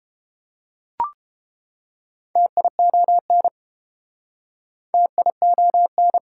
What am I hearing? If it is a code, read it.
Morse code, heard as a keyed tone, about 31 wpm, 700 Hz. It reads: TION TION